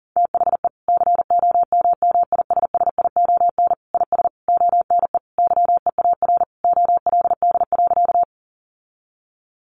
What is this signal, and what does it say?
Morse code, heard as a keyed tone, about 40 wpm, 700 Hz. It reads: THE COMMISSION IS ONE YEAR OLD.
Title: Sentences from Top 1400 Words 40wpm